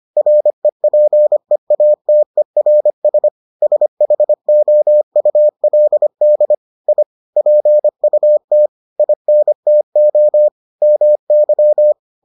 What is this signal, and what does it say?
Morse code, heard as a keyed tone, about 25 wpm, 600 Hz. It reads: REPEATERS SHOULD I PUT INTO MY